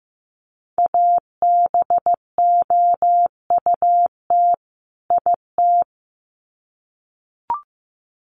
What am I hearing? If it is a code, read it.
Morse code, heard as a keyed tone, about 15 wpm, 700 Hz. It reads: ABOUT IT
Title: Sets of 2 Words 15wpm